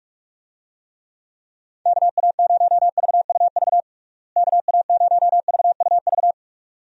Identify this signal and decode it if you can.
Morse code, heard as a keyed tone, about 46 wpm, 700 Hz. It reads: KA0VUV KA0VUV
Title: USA Call Signs - 6 Characters - Repeated 2x - 45wpm